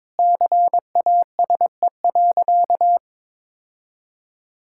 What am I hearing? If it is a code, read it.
Morse code, heard as a keyed tone, about 22 wpm, 700 Hz. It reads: CASE.
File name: practice / 22wpm / Sentences with 2-Word Phrases - 22wpm